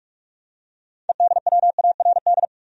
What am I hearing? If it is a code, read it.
Morse code, heard as a keyed tone, about 46 wpm, 700 Hz. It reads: EDWARD
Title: USA Common Names - 45wpm